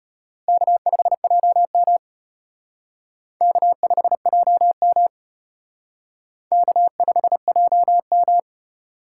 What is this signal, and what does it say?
Morse code, heard as a keyed tone, about 31 wpm, 700 Hz. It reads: K5JM K5JM K5JM
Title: Speed Racing - USA 1x2 Call Signs 25wpm